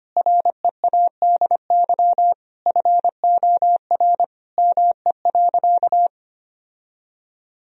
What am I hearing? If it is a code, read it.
Morse code, heard as a keyed tone, about 25 wpm, 700 Hz. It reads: READY FOR ME.
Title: Sentences from Top 1000 Words - Review Part 1 - 25wpm